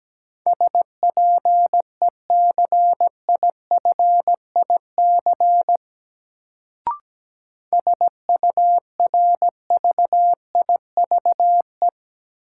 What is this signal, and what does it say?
Morse code, heard as a keyed tone, about 17 wpm, 700 Hz. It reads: SPECIFIC SURVIVE
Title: Sentences from Top 1300 Words - New Word Drill Alt - 17wpm